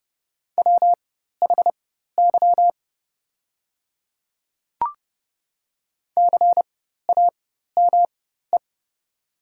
Morse code, heard as a keyed tone, about 30 wpm, spaced as for 14 wpm, 700 Hz. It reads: WHY CAME
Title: 4x Character Spacing - Top 300 Words - 30wpm